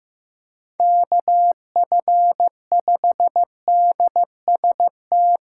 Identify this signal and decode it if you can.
Morse code, heard as a keyed tone, about 15 wpm, 700 Hz. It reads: KF5DST
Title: USA Callsign DE Callsign 15wpm